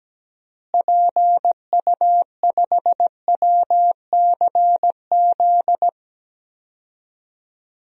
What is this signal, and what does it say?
Morse code, heard as a keyed tone, about 17 wpm, 700 Hz. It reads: PU5WCZ
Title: DXCC-100 Call Signs - 6 Characters - Repeated 3x - 17wpm